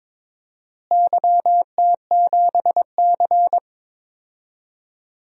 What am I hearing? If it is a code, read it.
Morse code, heard as a keyed tone, about 22 wpm, 700 Hz. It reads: YT7C
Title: DXCC-100 Call Signs - 4 Characters - Repeated 3x - 22wpm